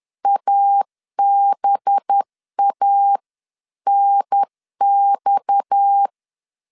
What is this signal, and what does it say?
Morse code, heard as a keyed tone, about 10 wpm, 800 Hz. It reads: ABANX
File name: CW / DET / 1 6 1500C S C 10wpm 4252